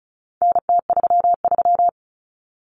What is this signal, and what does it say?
Morse code, heard as a keyed tone, about 34 wpm, 700 Hz. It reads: NT33
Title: Common Cut Numbers - 4-Digit 35wpm